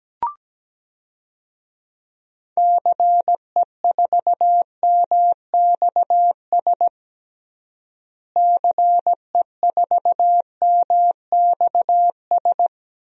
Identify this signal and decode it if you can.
Morse code, heard as a keyed tone, about 17 wpm, 700 Hz. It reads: CE4MXS CE4MXS
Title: DXCC-100 Call Signs - 6 Characters - Repeated 3x - 17wpm